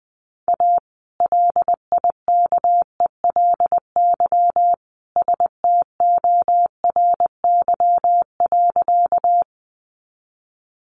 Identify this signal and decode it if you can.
Morse code, heard as a keyed tone, about 20 wpm, 700 Hz. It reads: A LIKELY STORY.
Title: Sentences from Top 600 Words 20wpm